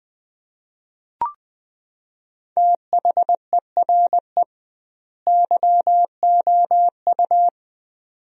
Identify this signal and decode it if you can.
Morse code, heard as a keyed tone, about 20 wpm, 700 Hz. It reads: THERE YOU
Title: 2x Word-Spacing - Sentences from Top 100 Words - 20wpm